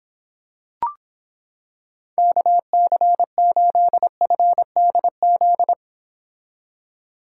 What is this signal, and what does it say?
Morse code, heard as a keyed tone, about 26 wpm, 700 Hz. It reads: KC8FDZ